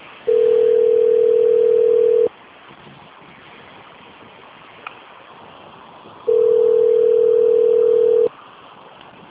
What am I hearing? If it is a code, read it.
Ringback tone (North America), 440 + 480 Hz.